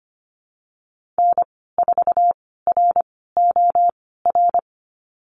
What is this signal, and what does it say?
Morse code, heard as a keyed tone, about 25 wpm, spaced as for 16 wpm, 690 Hz. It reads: N4ROR
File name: Lesson 15g - Number 5 - Call Signs 25wpm